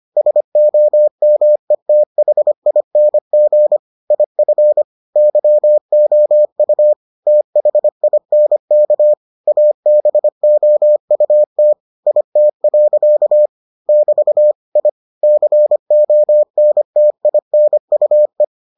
Morse code, heard as a keyed tone, about 25 wpm, 600 Hz. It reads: SOMETHING IF YOU THINK ABOUT IT. <BT> I CONTINUE